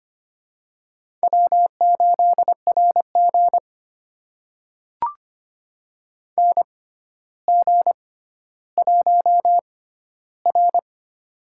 Morse code, heard as a keyed tone, about 25 wpm, spaced as for 13 wpm, 700 Hz. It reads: W8RG NG1R
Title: USA Call Signs - 4 Characters - 6x Character Spacing - 25wpm